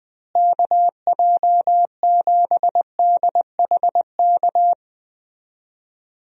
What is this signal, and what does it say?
Morse code, heard as a keyed tone, about 21 wpm, 700 Hz. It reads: KJ7DHK
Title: USA Call Signs - 6 Characters - Repeated 2x - 20wpm